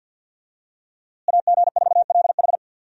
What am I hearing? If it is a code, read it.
Morse code, heard as a keyed tone, about 52 wpm, 700 Hz. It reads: AG4LH
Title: USA Call Signs - 5 Characters - Repeated 3x - 50wpm